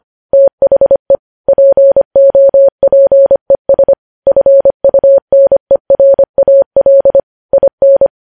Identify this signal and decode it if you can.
Morse code, heard as a keyed tone, about 25 wpm, 550 Hz. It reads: THE POPES FUNERAL IN
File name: podcast / News headlines-1x-25x25